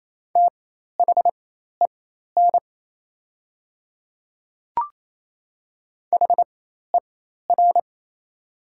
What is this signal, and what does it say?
Morse code, heard as a keyed tone, about 28 wpm, spaced as for 13 wpm, 700 Hz. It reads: THEN HER
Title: 4x Character Spacing - Top 100 Words - 28wpm